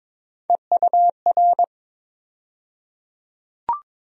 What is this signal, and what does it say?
Morse code, heard as a keyed tone, about 22 wpm, 700 Hz. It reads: EUR
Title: USA Call Sign Postfixes - 3 Letters 22wpm